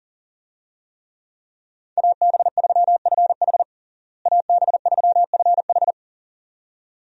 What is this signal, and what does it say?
Morse code, heard as a keyed tone, about 40 wpm, 700 Hz. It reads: AB3FH AB3FH
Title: USA Call Signs - 5 Characters - Repeated 2x - 40wpm